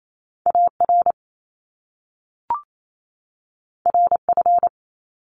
Morse code, heard as a keyed tone, about 28 wpm, 700 Hz. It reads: AR RF
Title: Top Two Letter Combinations in Words 28wpm